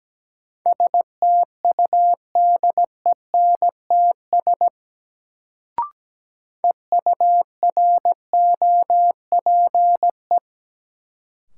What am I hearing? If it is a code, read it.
Morse code, heard as a keyed tone, about 17 wpm, 700 Hz. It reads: STUDENTS EUROPE
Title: Sets of 5 Words Encore - New Word Drill - 17wpm